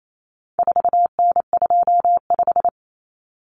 Morse code, heard as a keyed tone, about 28 wpm, 700 Hz. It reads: 4N25
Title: Common Cut Numbers - 4-Digit 28wpm